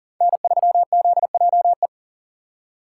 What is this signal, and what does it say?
Morse code, heard as a keyed tone, about 40 wpm, 700 Hz. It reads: N3ZJE
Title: USA Call Signs - 5 Characters - 4x Character Spacing - 40wpm